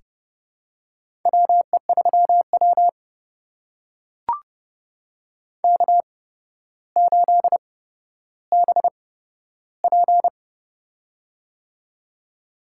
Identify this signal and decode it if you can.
Morse code, heard as a keyed tone, about 31 wpm, spaced as for 13 wpm, 700 Hz. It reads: WE3W K8BP
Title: USA Call Signs - 4 Characters - 8x Character Spacing - 30wpm